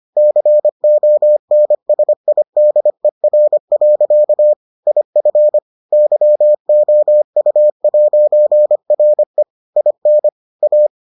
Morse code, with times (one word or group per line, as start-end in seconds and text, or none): 0.2-4.5 CONSIDER.
4.9-5.6 IF
5.9-9.4 YOU'RE
9.8-10.3 IN
10.6-10.9 A